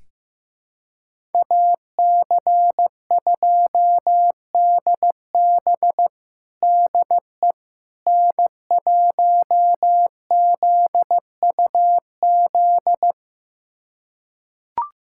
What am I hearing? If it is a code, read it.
Morse code, heard as a keyed tone, about 15 wpm, 700 Hz. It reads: AC2DB DE N1ZUZ
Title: USA Callsign DE Callsign - 5 Characters - 15wpm